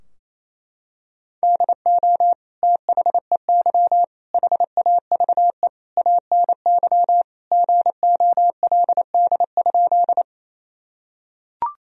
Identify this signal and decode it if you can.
Morse code, heard as a keyed tone, about 28 wpm, 700 Hz. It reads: DO THEY HAVE ANY GOLD?
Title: Sentences from Top 1000 Words - Review Part 3 - 28wpm